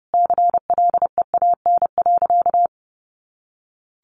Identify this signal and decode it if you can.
Morse code, heard as a keyed tone, about 30 wpm, 700 Hz. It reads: CLEAN.
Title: Sentences from Top 1100 Words 30wpm